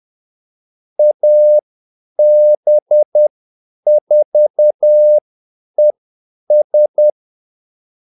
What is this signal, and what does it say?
Morse code, heard as a keyed tone, about 10 wpm, spaced as for 8 wpm, 600 Hz. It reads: AB4ES
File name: Callsigns-10-8wpm